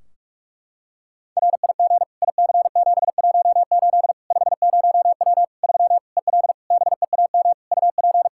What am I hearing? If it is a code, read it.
Morse code, heard as a keyed tone, about 41 wpm, 700 Hz. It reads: RIG IC718 50W 3 EL BEAM UP
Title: Practice QSO - Exchange 2 - Info - 45wpm